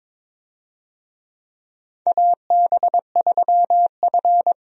Morse code, heard as a keyed tone, about 22 wpm, 700 Hz. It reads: AB3F
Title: USA Call Signs - 4 Characters - 22wpm